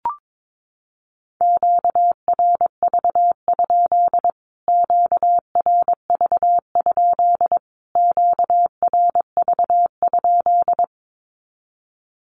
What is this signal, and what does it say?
Morse code, heard as a keyed tone, about 22 wpm, 700 Hz. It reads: QRV? QRV? QRV?
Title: Q-Codes Repeated 3x 22wpm